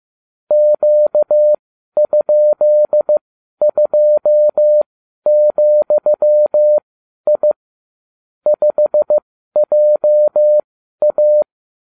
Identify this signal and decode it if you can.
Morse code, heard as a keyed tone, about 15 wpm, spaced as for 11 wpm, 600 Hz. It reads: Q?2,I 5JA